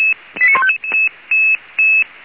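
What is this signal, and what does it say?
Five-tone selective call (ZVEI), heard as quick runs of five tones, tones of about 70 ms. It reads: 89155
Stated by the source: ZVEI five-tone selective call received by SDR, for Feuerwehr Orlamünde fire brigade.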